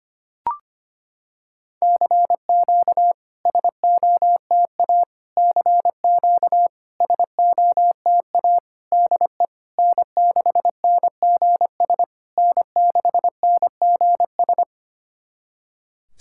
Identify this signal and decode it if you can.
Morse code, heard as a keyed tone, about 25 wpm, 700 Hz. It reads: CQ SOTA CQ SOTA DE N6NGS N6NGS
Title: SOTA QSO Practice 25wpm